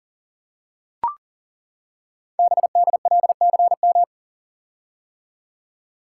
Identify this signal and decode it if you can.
Morse code, heard as a keyed tone, about 41 wpm, 700 Hz. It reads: BDLCM
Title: Code Groups - 5 Letters - 40wpm